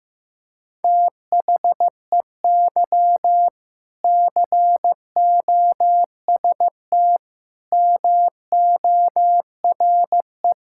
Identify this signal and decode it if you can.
Morse code, heard as a keyed tone, about 15 wpm, 700 Hz. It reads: THEY COST MORE
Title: Sentences from Top 1000 Words - Review Part 5 - 15wpm